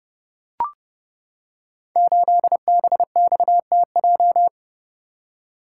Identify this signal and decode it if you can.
Morse code, heard as a keyed tone, about 29 wpm, 700 Hz. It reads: 8BXTJ
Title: Code Groups - 5 Letter:Numbers - 30wpm